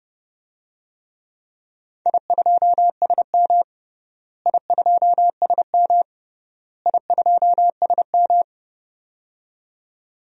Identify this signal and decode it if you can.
Morse code, heard as a keyed tone, about 31 wpm, 700 Hz. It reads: I2SM I2SM I2SM
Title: DXCC-100 Call Signs - 4 Characters - Repeated 3x - 30wpm